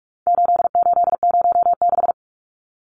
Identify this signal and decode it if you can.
Morse code, heard as a keyed tone, about 44 wpm, 700 Hz. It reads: 8806
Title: Numbers - 4-Digit 45wpm